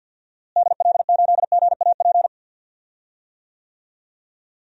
Morse code, heard as a keyed tone, about 45 wpm, 700 Hz. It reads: DL8GAP
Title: DXCC-100 Call Signs - 6 Characters - Repeated 2x - 50wpm